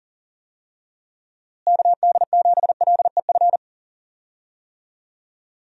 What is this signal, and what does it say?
Morse code, heard as a keyed tone, about 40 wpm, 700 Hz. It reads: KD7LEF
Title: USA Call Signs - 6 Characters - Repeated 2x - 40wpm